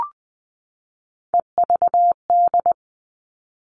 Morse code, heard as a keyed tone, about 20 wpm, 700 Hz. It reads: EVD
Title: All Cut Numbers - 3-Digit 20wpm